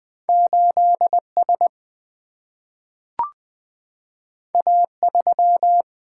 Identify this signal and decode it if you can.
Morse code, heard as a keyed tone, about 20 wpm, 700 Hz. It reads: 8 S A 3
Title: Single-Letter Single-Number 20wpm